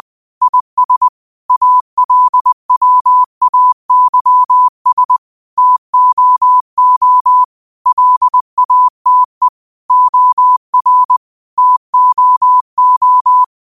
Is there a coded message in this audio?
Yes: Morse code from 0.4 to 13.4 s, heard as a keyed tone, about 20 wpm, 1000 Hz. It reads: IS ALWAYS TOO LATE OR TOO